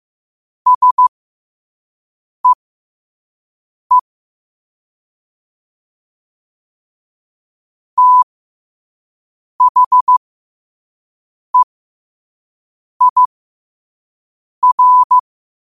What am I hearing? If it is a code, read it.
Morse code, heard as a keyed tone, about 15 wpm, spaced as for 5 wpm, 1000 Hz. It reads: SEE THEIR
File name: qotdfiles / 5wpm